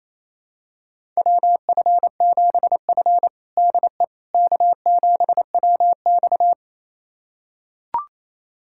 Morse code, heard as a keyed tone, about 27 wpm, 700 Hz. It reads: WF7F DE K7WX
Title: USA Callsign DE Callsign - 4 Characters - 28wpm